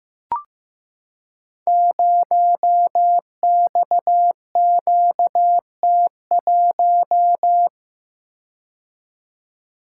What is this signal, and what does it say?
Morse code, heard as a keyed tone, about 15 wpm, 700 Hz. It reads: 0XQT1
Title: Code Groups - 5 Letter:Numbers - 15wpm